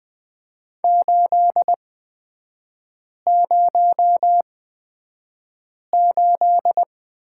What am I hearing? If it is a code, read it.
Morse code, heard as a keyed tone, about 20 wpm, spaced as for 5 wpm, 700 Hz. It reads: 808
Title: 243 Lesson 39e - Letter z - Three Characters 20wpm